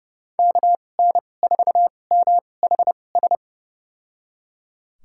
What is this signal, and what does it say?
Morse code, heard as a keyed tone, about 30 wpm, spaced as for 22 wpm, 700 Hz. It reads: KN4MHS
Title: USA Call Signs - 2x Character Spacing - 30wpm